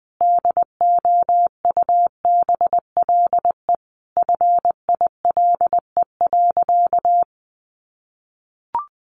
Morse code, heard as a keyed tone, about 20 wpm, 700 Hz. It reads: DOUBLE FILE.
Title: Sentences from Top 1400 Words 20wpm